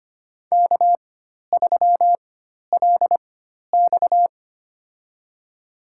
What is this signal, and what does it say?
Morse code, heard as a keyed tone, about 25 wpm, spaced as for 12 wpm, 690 Hz. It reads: K3LX
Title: USA Call Signs - 4 Characters - 4x Character Spacing - 25wpm